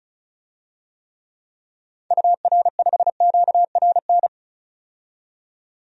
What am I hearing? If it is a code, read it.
Morse code, heard as a keyed tone, about 34 wpm, 700 Hz. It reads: UR5QRN
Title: DXCC-100 Call Signs - 6 Characters - 35wpm